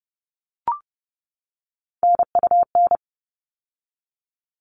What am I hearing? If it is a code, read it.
Morse code, heard as a keyed tone, about 30 wpm, 700 Hz. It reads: NUN
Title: All Cut Numbers - 3-Digit 30wpm